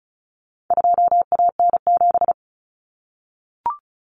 Morse code, heard as a keyed tone, about 34 wpm, 700 Hz. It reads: 2AN7
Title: Common Cut Numbers - 4-Digit 35wpm